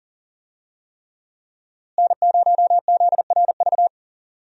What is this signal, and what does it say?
Morse code, heard as a keyed tone, about 40 wpm, 700 Hz. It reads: N0ZRV
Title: USA Call Signs - 5 Characters - Repeated 3x - 40wpm